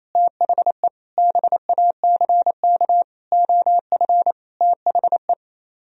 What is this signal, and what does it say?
Morse code, heard as a keyed tone, about 33 wpm, 700 Hz. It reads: THE BACK OF THE
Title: Speed-Racing - Sets of 5 Words 28wpm